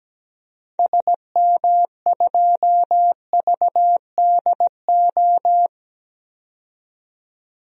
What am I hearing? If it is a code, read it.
Morse code, heard as a keyed tone, about 17 wpm, 700 Hz. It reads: SM2VDO
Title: DXCC-100 Call Signs - 6 Characters - 17wpm